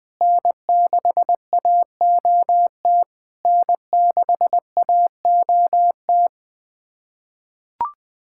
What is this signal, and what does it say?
Morse code, heard as a keyed tone, about 20 wpm, 690 Hz. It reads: N6AOT N6AOT
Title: SOTA QSO Practice 20wpm